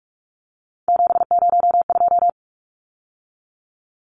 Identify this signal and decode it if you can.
Morse code, heard as a keyed tone, about 44 wpm, 700 Hz. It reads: 702
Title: Numbers - 3-Digit 45wpm